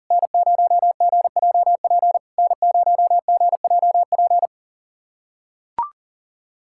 Morse code, heard as a keyed tone, about 39 wpm, 690 Hz. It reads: N0GJP N0GJP
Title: SOTA QSO Practice 40wpm